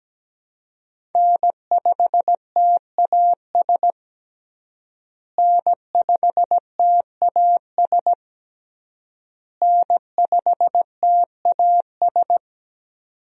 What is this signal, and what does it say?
Morse code, heard as a keyed tone, about 17 wpm, 700 Hz. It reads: N5TAS N5TAS N5TAS
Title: USA Call Signs - 5 Characters - Repeated 3x - 17wpm